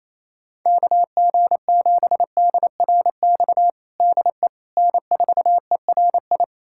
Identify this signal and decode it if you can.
Morse code, heard as a keyed tone, about 28 wpm, 700 Hz. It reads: KG7DRX DE N4ERI
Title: USA Callsign DE Callsign 28wpm